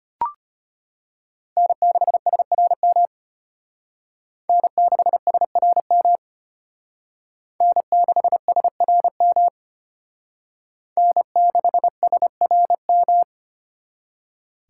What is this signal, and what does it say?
Morse code, heard as a keyed tone, about 32 wpm, 700 Hz. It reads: N6SRM N6SRM N6SRM N6SRM